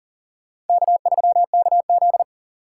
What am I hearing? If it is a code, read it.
Morse code, heard as a keyed tone, about 51 wpm, 700 Hz. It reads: K3KZ